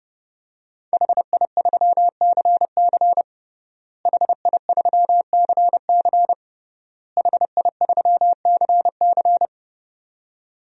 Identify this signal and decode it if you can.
Morse code, heard as a keyed tone, about 30 wpm, 700 Hz. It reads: HI3CC HI3CC HI3CC
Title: DXCC-100 Call Signs - 5 Characters - Repeated 3x - 30wpm